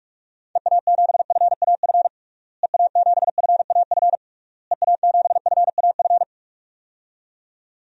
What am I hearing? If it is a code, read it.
Morse code, heard as a keyed tone, about 41 wpm, 700 Hz. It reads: EA7FAF EA7FAF EA7FAF
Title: DXCC-100 Call Signs - 6 Characters - Repeated 3x - 45wpm